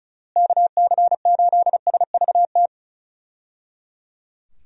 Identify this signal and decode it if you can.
Morse code, heard as a keyed tone, about 36 wpm, 700 Hz. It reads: KC8SVT
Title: USA Call Signs - 6 Characters - Repeated 3x - 35wpm